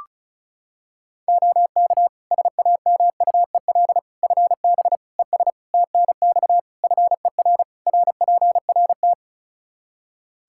Morse code, heard as a keyed tone, about 34 wpm, 700 Hz. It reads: OK SAMUEL FB ES TNX FER RPRT
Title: K1USN SST QSO Practice 35wpm